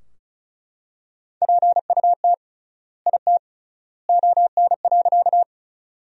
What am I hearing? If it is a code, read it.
Morse code, heard as a keyed tone, about 36 wpm, 700 Hz. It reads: PUT IT ON.
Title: 3x Word-Spacing - Sentences from Top 200 Words - 35wpm